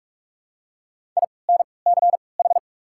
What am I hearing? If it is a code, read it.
Morse code, heard as a keyed tone, about 43 wpm, 700 Hz. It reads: I N C H
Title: 3x Character Spacing - Top 500 Words - 45wpm